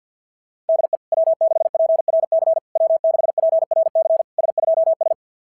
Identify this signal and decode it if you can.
Morse code, heard as a keyed tone, about 47 wpm, 660 Hz. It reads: DE W6PRC W6PRC S2S
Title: SOTA QSO Practice 50wpm